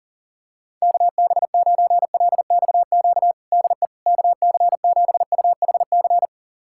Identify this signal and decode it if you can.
Morse code, heard as a keyed tone, about 40 wpm, 700 Hz. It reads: KB9LXQ DE KC7UHC
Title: USA Callsign DE Callsign - 6 Characters - 40wpm